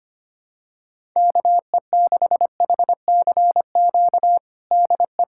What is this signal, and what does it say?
Morse code, heard as a keyed tone, about 25 wpm, 700 Hz. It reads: KE6HCQ DE
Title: USA Callsign DE Callsign 25wpm